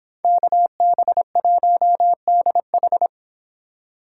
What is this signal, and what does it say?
Morse code, heard as a keyed tone, about 25 wpm, 700 Hz. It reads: KB1DH